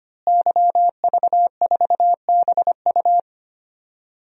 Morse code, heard as a keyed tone, about 25 wpm, 700 Hz. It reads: YV4BU
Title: DXCC-100 Call Signs - 5 Characters - Repeated 3x - 25wpm